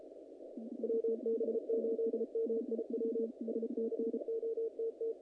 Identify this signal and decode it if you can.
Morse code, heard as a keyed tone, about 36 wpm, 450 Hz, with noise. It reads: WB8MIW SCOTT
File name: contests / naqp-cw-k5tr-op-k5pi- -2100